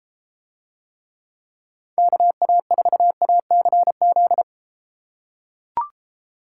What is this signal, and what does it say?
Morse code, heard as a keyed tone, about 30 wpm, 700 Hz. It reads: KA4ACZ